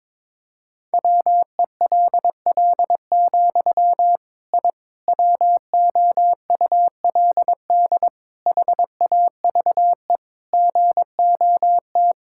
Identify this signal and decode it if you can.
Morse code, heard as a keyed tone, about 23 wpm, 700 Hz. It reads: WELL, I WOULD HAVE GOT
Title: Sentences from Top 1000 Words - Review Part 5 - 22wpm